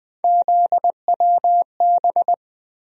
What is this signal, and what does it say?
Morse code, heard as a keyed tone, about 20 wpm, 700 Hz. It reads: ZWB
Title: USA Call Sign Postfixes - 3 Letters 20wpm